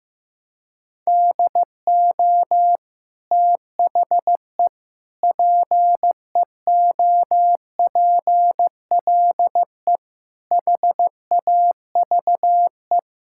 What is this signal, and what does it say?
Morse code, heard as a keyed tone, about 16 wpm, 700 Hz. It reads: DO THE PEOPLE HAVE
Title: Sentences from Top 1000 Words - Review Part 5 - 15wpm